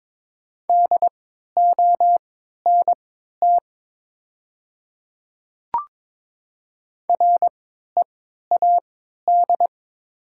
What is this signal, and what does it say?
Morse code, heard as a keyed tone, about 22 wpm, spaced as for 12 wpm, 700 Hz. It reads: DONT READ
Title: 3x Character Spacing - Top 300 Words - 22wpm